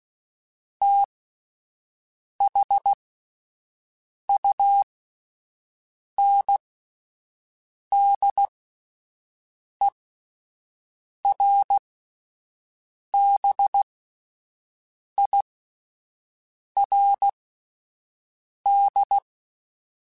Morse code, heard as a keyed tone, about 16 wpm, spaced as for 5 wpm, 800 Hz. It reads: THUNDERBIRD
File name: (spaced) / QSOs Spaced-mp3s / WD0BGP 1 @10 sp